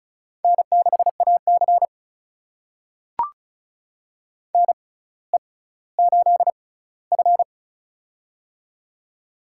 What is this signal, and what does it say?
Morse code, heard as a keyed tone, about 36 wpm, spaced as for 18 wpm, 700 Hz. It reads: N6AC NE8F
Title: USA Call Signs - 4 Characters - 6x Character Spacing - 35wpm